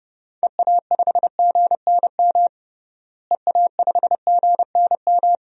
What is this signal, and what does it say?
Morse code, heard as a keyed tone, about 31 wpm, 700 Hz. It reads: EA5GNM EA5GNM
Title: DXCC-100 Call Signs - 6 Characters - Repeated 2x - 30wpm